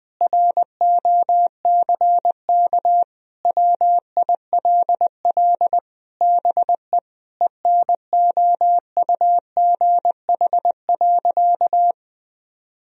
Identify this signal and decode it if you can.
Morse code, heard as a keyed tone, about 20 wpm, 700 Hz. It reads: ROCK WILL BE ENOUGH.
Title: Sentences from Top 1000 Words - Review Part 2 - 20wpm